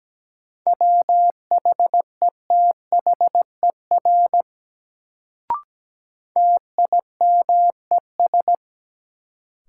Morse code, heard as a keyed tone, about 17 wpm, 700 Hz. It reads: WHETHER TIMES
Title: Sets of 3 Words - New Word Drill - 17wpm